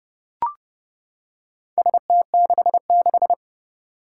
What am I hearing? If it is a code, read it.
Morse code, heard as a keyed tone, about 30 wpm, 700 Hz. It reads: ST66
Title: Four Challenging Letter-Number Combinations 30wpm